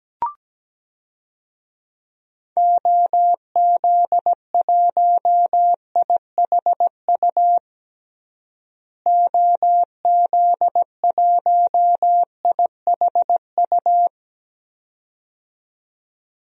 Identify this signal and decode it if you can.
Morse code, heard as a keyed tone, about 17 wpm, 700 Hz. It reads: OZ1IHU OZ1IHU
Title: DXCC-100 Call Signs - 6 Characters - Repeated 2x - 17wpm